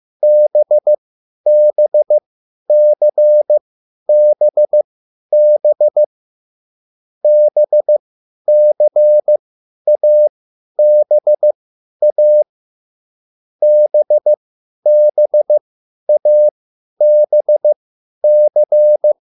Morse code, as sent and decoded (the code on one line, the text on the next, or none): -... -... -.-. -... -... / -... -.-. .- -... .- / -... -... .- -... -.-.
BBCBB BCABA BBABC